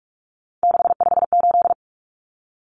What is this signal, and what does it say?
Morse code, heard as a keyed tone, about 44 wpm, 700 Hz. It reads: 658
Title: Numbers - 3-Digit 45wpm